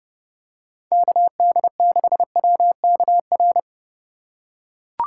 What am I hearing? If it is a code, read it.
Morse code, heard as a keyed tone, about 29 wpm, 700 Hz. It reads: KD6WKR